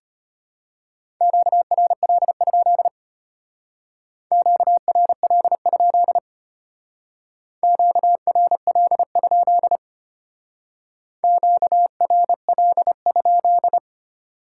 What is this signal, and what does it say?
Morse code, heard as a keyed tone, about 31 wpm, 700 Hz. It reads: QRL? QRL? QRL? QRL?